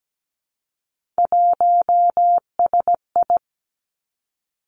Morse code, heard as a keyed tone, about 17 wpm, 700 Hz. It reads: 1SI
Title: Three Challenging Letter-Number Combinations 17wpm